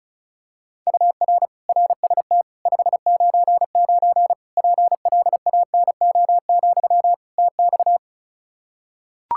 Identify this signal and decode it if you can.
Morse code, heard as a keyed tone, about 34 wpm, 700 Hz. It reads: UR RST 599 PLANO, TX
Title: K1USN SST QSO Practice 35wpm